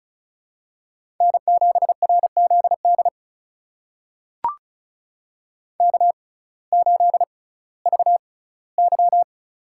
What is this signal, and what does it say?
Morse code, heard as a keyed tone, about 36 wpm, spaced as for 19 wpm, 700 Hz. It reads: N7RZD K8VY
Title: USA Call Signs - 6x Character Spacing - 35wpm